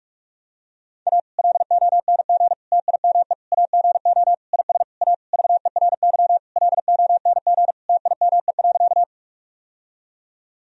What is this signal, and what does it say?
Morse code, heard as a keyed tone, about 45 wpm, 700 Hz. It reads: A LONG TIME AGO IS A VERY LONG TIME.
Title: Sentences from Top 1000 Words - Review Part 3 - 45wpm